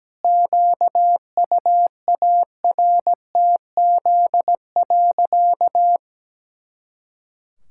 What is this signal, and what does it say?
Morse code, heard as a keyed tone, about 17 wpm, 700 Hz. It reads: QUARTZ.
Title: Pangram Challenge v2 - 17wpm